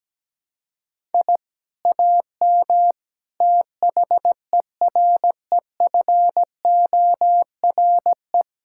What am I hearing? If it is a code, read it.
Morse code, heard as a keyed tone, about 17 wpm, 700 Hz. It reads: I AM THEREFORE
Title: Sentences from Top 1000 Words - Review Part 6 - 17wpm